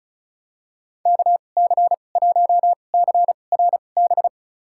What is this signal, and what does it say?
Morse code, heard as a keyed tone, about 36 wpm, spaced as for 25 wpm, 700 Hz. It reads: KC1CRB
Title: USA Call Signs - 2x Character Spacing - 35wpm